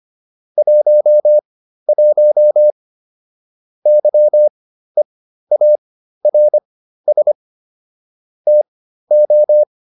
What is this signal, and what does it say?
Morse code, heard as a keyed tone, about 25 wpm, spaced as for 13 wpm, 600 Hz. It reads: 11 YEARS TO